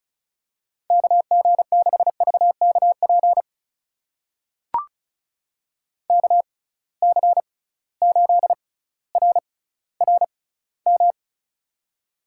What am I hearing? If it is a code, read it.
Morse code, heard as a keyed tone, about 36 wpm, spaced as for 18 wpm, 700 Hz. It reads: KG6VKP KC8RRM
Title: USA Call Signs - 6 Characters - 6x Character Spacing - 35wpm